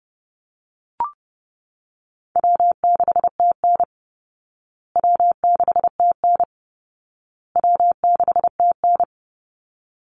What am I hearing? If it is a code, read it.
Morse code, heard as a keyed tone, about 30 wpm, 700 Hz. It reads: W6TN W6TN W6TN
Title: Easy Call Signs Repeated 3x - 30wpm